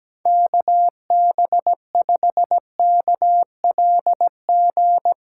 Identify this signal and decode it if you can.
Morse code, heard as a keyed tone, about 17 wpm, 700 Hz. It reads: KB5KLG
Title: USA Callsign DE Callsign 17wpm